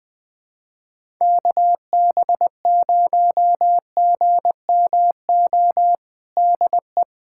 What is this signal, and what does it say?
Morse code, heard as a keyed tone, about 20 wpm, 700 Hz. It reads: KB0GMO DE
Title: USA Callsign DE Callsign 20wpm